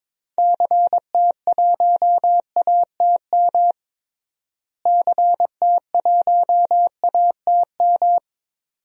Morse code, heard as a keyed tone, about 21 wpm, 700 Hz. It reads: CT1ATM CT1ATM
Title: DXCC-100 Call Signs - 6 Characters - Repeated 2x - 22wpm